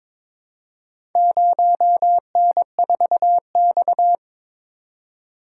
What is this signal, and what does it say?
Morse code, heard as a keyed tone, about 22 wpm, 700 Hz. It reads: 0N4X
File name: practice / 22wpm / Four Letter-Number Combinations - 22wpm